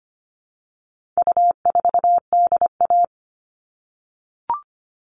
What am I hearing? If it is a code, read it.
Morse code, heard as a keyed tone, about 25 wpm, 700 Hz. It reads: U4DA
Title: All Cut Numbers - 4-Digit 25wpm